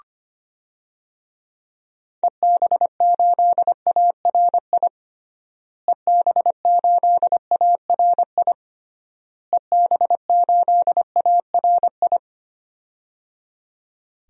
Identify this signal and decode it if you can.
Morse code, heard as a keyed tone, about 25 wpm, 700 Hz. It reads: EB8ARI EB8ARI EB8ARI
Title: DXCC-100 Call Signs - 6 Characters - Repeated 3x - 25wpm